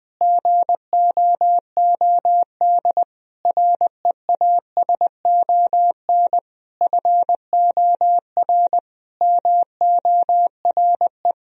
Morse code, heard as a keyed tone, about 20 wpm, 700 Hz. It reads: GOOD REASON FOR MORE
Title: Sentences from Top 1000 Words - Review Part 4 - 20wpm